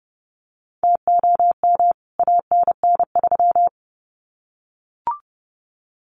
Morse code, heard as a keyed tone, about 30 wpm, 700 Hz. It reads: TOM ANN3
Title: CWT Exchange 30wpm